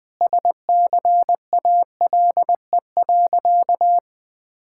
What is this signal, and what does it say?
Morse code, heard as a keyed tone, about 20 wpm, 700 Hz. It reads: SCALE.
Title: Sentences from Top 1000 Words - Review Part 5 - 20wpm